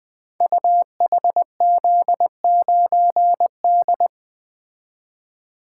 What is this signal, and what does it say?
Morse code, heard as a keyed tone, about 20 wpm, 700 Hz. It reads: UHZ9D
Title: Code Groups - 5 Letter:Numbers - 20wpm